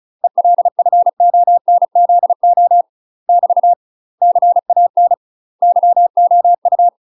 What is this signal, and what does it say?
Morse code, heard as a keyed tone, about 36 wpm, 700 Hz. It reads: ELFONZO <BT> CAN YOU
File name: UK MASTER QSOs TRAINING / 35WPM / UK MASTER QSOs 0013